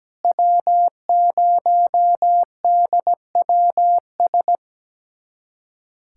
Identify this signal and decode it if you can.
Morse code, heard as a keyed tone, about 17 wpm, 700 Hz. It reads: W0DWS
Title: USA Callsign DE Callsign - 5 Characters - 17wpm